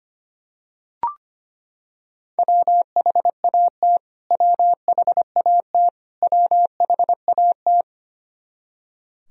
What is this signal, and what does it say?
Morse code, heard as a keyed tone, about 25 wpm, 700 Hz. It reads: WHAT WHAT WHAT
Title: Top 100 Words Repeated 25wpm